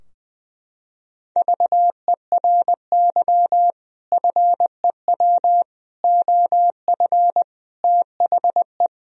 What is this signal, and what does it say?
Morse code, heard as a keyed tone, about 20 wpm, 700 Hz. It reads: VERY FEW OF THE
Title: Sentences from Top 1000 Words - Review Part 3 - 20wpm